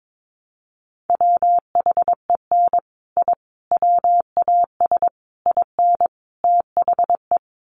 Morse code, heard as a keyed tone, about 22 wpm, 700 Hz. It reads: WHEN I WAS IN THE
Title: Sets of 5 Words Encore2 22wpm